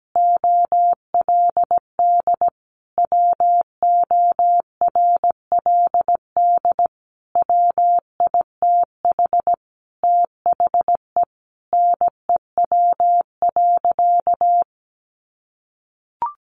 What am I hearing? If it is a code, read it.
Morse code, heard as a keyed tone, about 17 wpm, 700 Hz. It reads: OLD WORLD WITH THE NEW.
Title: Sentences from Top 1300 Words 17wpm